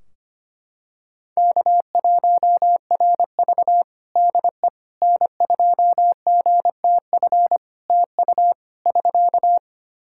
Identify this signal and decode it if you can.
Morse code, heard as a keyed tone, about 25 wpm, 700 Hz. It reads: K1RV DE N2GTF TU <SK>
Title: Practice QSO - Exchange 4 - Ending - 25wpm